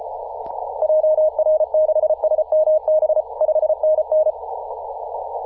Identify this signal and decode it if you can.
Morse code, heard as a keyed tone, about 34 wpm, 620 Hz. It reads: JR6SMB 5NN
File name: posts / 2017-aug-07-0652z-ri0li-14-036mhz-cw